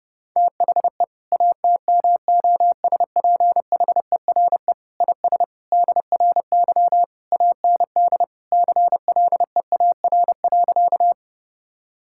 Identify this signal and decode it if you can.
Morse code, heard as a keyed tone, about 30 wpm, 700 Hz. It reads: THE ATMOSPHERE IS DRY AND CLEAR.
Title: Sentences from Top 1600 Words 30wpm